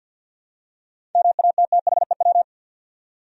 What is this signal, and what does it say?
Morse code, heard as a keyed tone, about 46 wpm, 700 Hz. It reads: MATTHEW
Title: USA Common Names - 50wpm